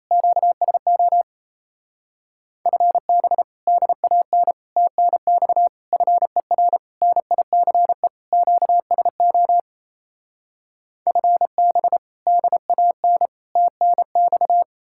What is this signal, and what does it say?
Morse code, heard as a keyed tone, about 37 wpm, 700 Hz. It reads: QSO FB DAN TNX FER NICE QSO FB DAN TNX